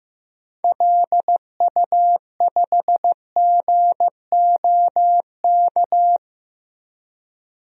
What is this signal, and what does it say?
Morse code, heard as a keyed tone, about 15 wpm, 700 Hz. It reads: LU5GOK
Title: DXCC-100 Call Signs - 6 Characters - Repeated 3x - 15wpm